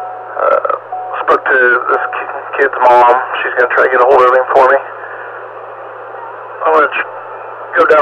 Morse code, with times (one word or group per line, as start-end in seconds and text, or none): none